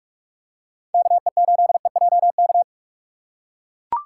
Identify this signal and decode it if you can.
Morse code, heard as a keyed tone, about 44 wpm, 700 Hz. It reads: KE8EJK